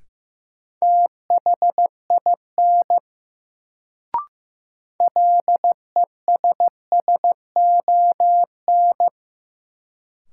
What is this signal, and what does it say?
Morse code, heard as a keyed tone, about 15 wpm, 700 Hz. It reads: THIN LESSON
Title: Sentences from Top 1200 Words - New Word Drill - 15wpm